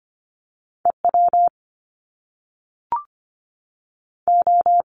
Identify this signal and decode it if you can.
Morse code, heard as a keyed tone, about 25 wpm, 700 Hz. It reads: E W O
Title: Top Two Letter Combinations in Words 25wpm